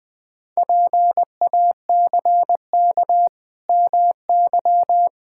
Morse code, heard as a keyed tone, about 20 wpm, 700 Hz. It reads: PACK MY
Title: Pangram Challenge v2 - 20wpm